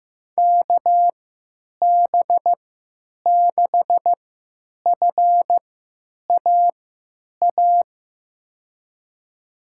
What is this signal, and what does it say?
Morse code, heard as a keyed tone, about 15 wpm, spaced as for 9 wpm, 700 Hz. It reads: KB6FAA
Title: USA Call Signs - 6 Characters - 3x Character Spacing - 15wpm